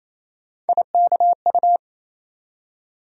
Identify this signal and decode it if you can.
Morse code, heard as a keyed tone, about 28 wpm, 700 Hz. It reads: IKU
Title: USA Call Sign Postfixes - 3 Letters 28wpm